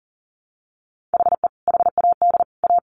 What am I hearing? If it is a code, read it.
Morse code, heard as a keyed tone, about 40 wpm, 700 Hz. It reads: HE HAD A
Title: Sentences from Top 1000 Words 40wpm